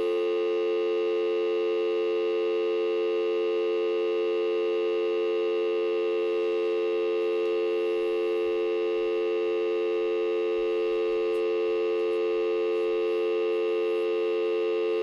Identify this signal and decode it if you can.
Dial tone (North America), 350 + 440 Hz, with noise.
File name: fios dv dial tone timeout